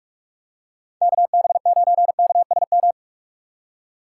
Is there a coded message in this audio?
Yes: Morse code from 1.0 to 2.9 s, heard as a keyed tone, about 44 wpm, 700 Hz. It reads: KB9KSM